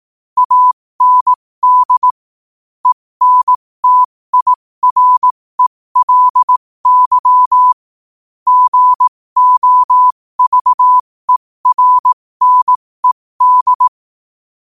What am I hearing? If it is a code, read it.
Morse code, heard as a keyed tone, about 18 wpm, spaced as for 15 wpm, 1000 Hz. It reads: AND ENTIRELY GOVERNED
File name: qotdfiles / 15wpm